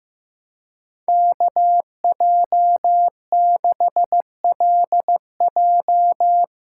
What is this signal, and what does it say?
Morse code, heard as a keyed tone, about 15 wpm, 700 Hz. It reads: KJ6LJ
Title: USA Callsign DE Callsign - 5 Characters - 15wpm